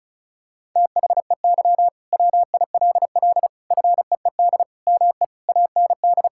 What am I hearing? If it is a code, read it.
Morse code, heard as a keyed tone, about 34 wpm, 700 Hz. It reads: THEY WILL FEED ME AND
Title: Sentences from Top 1600 Words 35wpm